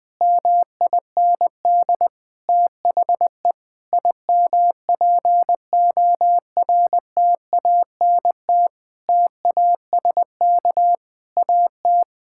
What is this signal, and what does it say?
Morse code, heard as a keyed tone, about 20 wpm, 700 Hz. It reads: MIND THE IMPORTANT TASK AT
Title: Sentences from Top 1600 Words 20wpm